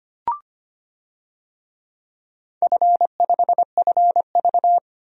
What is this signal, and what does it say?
Morse code, heard as a keyed tone, about 25 wpm, 700 Hz. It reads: F5FV
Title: DXCC-100 Call Signs - 4 Characters - 25wpm